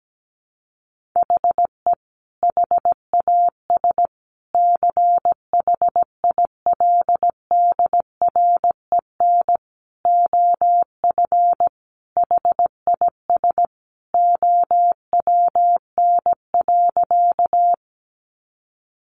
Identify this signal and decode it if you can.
Morse code, heard as a keyed tone, about 17 wpm, 700 Hz. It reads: HE HAS CHILDREN OF HIS OWN.
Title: Sentences from Top 300 Words 17wpm